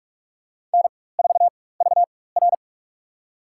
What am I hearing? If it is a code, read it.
Morse code, heard as a keyed tone, about 46 wpm, spaced as for 21 wpm, 700 Hz. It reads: N4VR
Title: USA Call Signs - 4 Characters - 4x Character Spacing - 45wpm